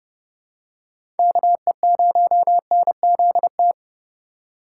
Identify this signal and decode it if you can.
Morse code, heard as a keyed tone, about 30 wpm, 700 Hz. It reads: KE0NZT
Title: USA Call Signs - 6 Characters - 6x Character Spacing - 30wpm